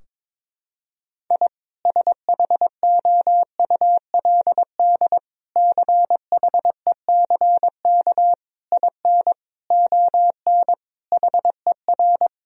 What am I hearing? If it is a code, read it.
Morse code, heard as a keyed tone, about 21 wpm, 700 Hz. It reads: I SHOULD CHECK IN ON HER
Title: Sentences from Top 1000 Words - Review Part 3 - 22wpm